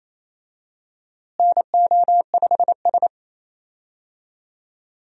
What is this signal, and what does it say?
Morse code, heard as a keyed tone, about 28 wpm, 700 Hz. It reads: NO5S
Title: USA Call Signs - 4 Characters - Repeated 2x - 28wpm